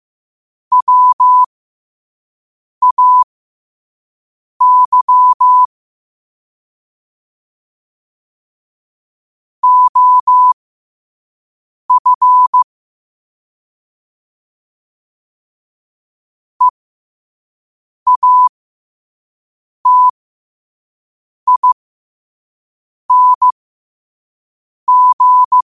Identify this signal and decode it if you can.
Morse code, heard as a keyed tone, about 15 wpm, spaced as for 5 wpm, 1000 Hz. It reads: WAY OF EATING
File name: 5wpm